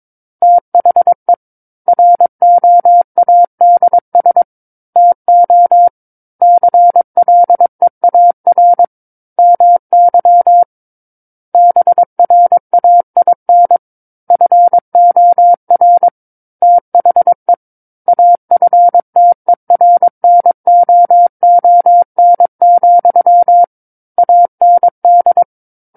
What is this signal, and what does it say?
Morse code, heard as a keyed tone, about 22 wpm, 700 Hz. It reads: THE ROADS TO CLEAR MY BRAIN FOR THE AFTERNOON, AND
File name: Book One / 13-Chapter 13 - 22 WPM